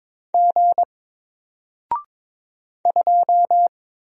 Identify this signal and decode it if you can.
Morse code, heard as a keyed tone, about 22 wpm, 700 Hz. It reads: G 2
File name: Single-Letter-Number-22wpm